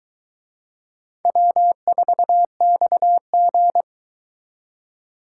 Morse code, heard as a keyed tone, about 23 wpm, 700 Hz. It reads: W4XG